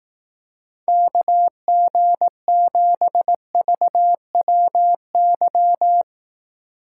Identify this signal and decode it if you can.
Morse code, heard as a keyed tone, about 18 wpm, 700 Hz. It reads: KG7VWY
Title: Speed Racing - USA 2x3 Call Signs 15wpm